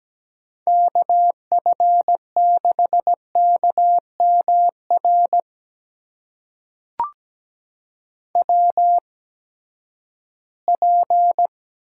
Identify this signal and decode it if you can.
Morse code, heard as a keyed tone, about 17 wpm, spaced as for 12 wpm, 700 Hz. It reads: KF6KMR WP
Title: USA Call Signs - 6 Characters - 8x Character Spacing - 17wpm